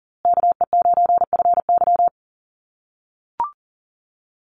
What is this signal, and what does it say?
Morse code, heard as a keyed tone, about 39 wpm, 700 Hz. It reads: KE9FY
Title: USA Call Signs Repeated 2x 40wpm